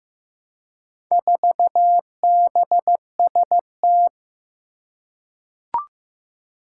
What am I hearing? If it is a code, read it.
Morse code, heard as a keyed tone, about 15 wpm, 700 Hz. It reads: 4BST
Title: Four Challenging Letter-Number Combinations 15wpm